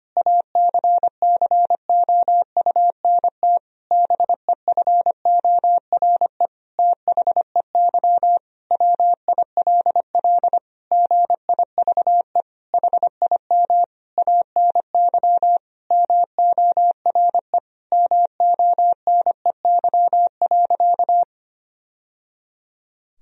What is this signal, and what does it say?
Morse code, heard as a keyed tone, about 25 wpm, 700 Hz. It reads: ACCOUNT BEFORE THEY WILL GIVE HIM ANY MORE MONEY.
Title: Sentences from Top 1000 Words - Review Part 4 - 25wpm